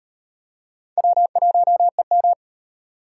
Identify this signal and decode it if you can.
Morse code, heard as a keyed tone, about 34 wpm, 700 Hz. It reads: W1EM